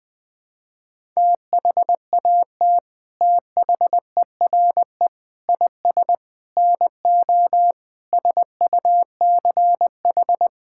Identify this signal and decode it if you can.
Morse code, heard as a keyed tone, about 20 wpm, 700 Hz. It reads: THAT THERE IS NO SUCH
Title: Sets of 8 Words - 20wpm